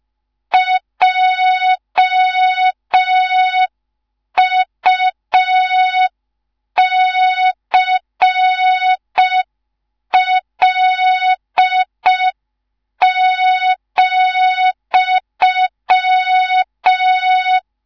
Morse code, heard as a keyed tone, about 5 wpm, 3700 Hz. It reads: JUCL,
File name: Morze-HA2MN / hang / 19 GYAKORLAT